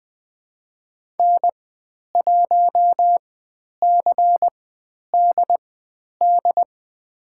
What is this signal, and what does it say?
Morse code, heard as a keyed tone, about 20 wpm, spaced as for 10 wpm, 700 Hz. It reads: N1CDD
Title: 095 Lesson 16g - Letter c - Call Signs 20wpm